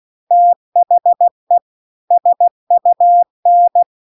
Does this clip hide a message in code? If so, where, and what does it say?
Yes: Morse code from 0.3 to 3.8 s, heard as a keyed tone, about 16 wpm, 700 Hz. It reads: THE SUN